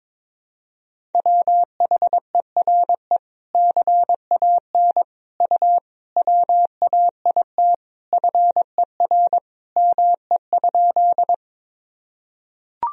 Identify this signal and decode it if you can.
Morse code, heard as a keyed tone, about 22 wpm, 700 Hz. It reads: WHERE CAN U WAIT FER ME?
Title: Sentences from Top 500 Words and Abbreviations 22wpm